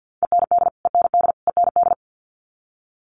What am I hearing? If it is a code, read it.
Morse code, heard as a keyed tone, about 50 wpm, 700 Hz. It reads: END END END
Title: Top 500 Words Repeated 50wpm